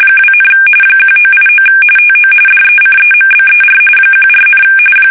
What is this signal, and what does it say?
RTTY, about 45.45 baud. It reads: 2, 1989 TO DECEMBER 14,
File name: fsk850 / battleshipnj